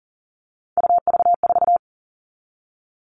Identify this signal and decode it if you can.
Morse code, heard as a keyed tone, about 40 wpm, 700 Hz. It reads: UV4
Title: All Cut Numbers - 3-Digit 40wpm